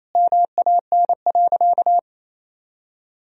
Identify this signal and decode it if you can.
Morse code, heard as a keyed tone, about 28 wpm, 700 Hz. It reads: MAN.
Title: Sentences from Top 200 Words and Abbreviations 28wpm